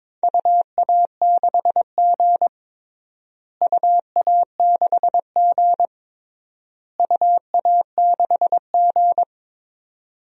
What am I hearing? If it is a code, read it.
Morse code, heard as a keyed tone, about 23 wpm, 700 Hz. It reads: UA6G UA6G UA6G
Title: DXCC-100 Call Signs - 4 Characters - Repeated 3x - 22wpm